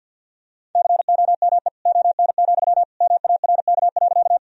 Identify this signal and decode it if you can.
Morse code, heard as a keyed tone, about 52 wpm, 700 Hz. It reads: COME ON, MARK.
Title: Sentences from Top 1000 Words - Review Part 1 - 50wpm